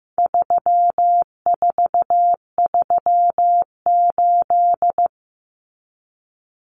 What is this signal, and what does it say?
Morse code, heard as a keyed tone, about 15 wpm, 700 Hz. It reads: 3438
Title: Numbers - 4-Digit 15wpm